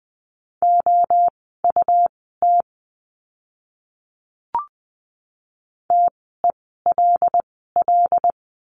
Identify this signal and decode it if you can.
Morse code, heard as a keyed tone, about 20 wpm, spaced as for 14 wpm, 700 Hz. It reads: OUT TELL
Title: 115 Lesson 19f - Letter w - Words 20wpm